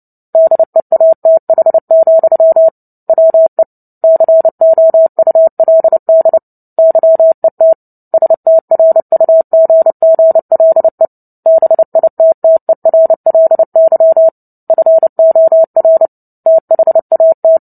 Morse code, heard as a keyed tone, about 30 wpm, 650 Hz. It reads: DEATH, WE COULD YET STRUGGLE BITTERLY FOR THAT
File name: Book Two / 03-Chapter 3 - 29 WPM